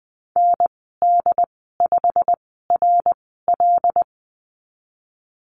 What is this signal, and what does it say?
Morse code, heard as a keyed tone, about 20 wpm, spaced as for 15 wpm, 700 Hz. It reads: ND5RL
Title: 088 Lesson 15g - Number 5 - Call Signs 20wpm